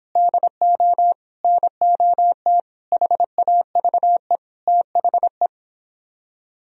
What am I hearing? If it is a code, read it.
Morse code, heard as a keyed tone, about 31 wpm, 700 Hz. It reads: DO NOT HAVE THE